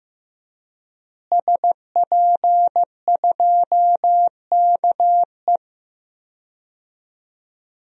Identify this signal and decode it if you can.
Morse code, heard as a keyed tone, about 15 wpm, 700 Hz. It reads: SP2KE
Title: DXCC-100 Call Signs - 5 Characters - Repeated 3x - 15wpm